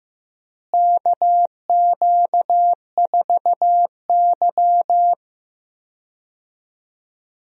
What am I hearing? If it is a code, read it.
Morse code, heard as a keyed tone, about 15 wpm, 700 Hz. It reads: KQ4Y
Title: USA Call Signs - 4 Characters - Repeated 2x - 15wpm